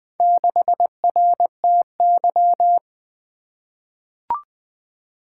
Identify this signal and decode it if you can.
Morse code, heard as a keyed tone, about 20 wpm, 700 Hz. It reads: 6RTY
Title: Four Letter-Number Combinations - 20wpm